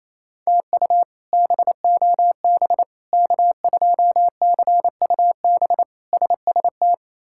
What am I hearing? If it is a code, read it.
Morse code, heard as a keyed tone, about 28 wpm, 700 Hz. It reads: TU BOB K2CUB SST
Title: K1USN SST QSO Practice - 28wpm